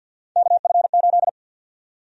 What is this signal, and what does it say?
Morse code, heard as a keyed tone, about 46 wpm, 700 Hz. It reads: KF8
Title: USA Call Sign Prefixes - Repeated 2x - 50wpm